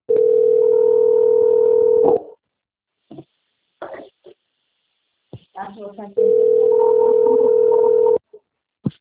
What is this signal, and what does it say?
Ringback tone (North America), 440 + 480 Hz, with speech.
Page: Araraquara - SP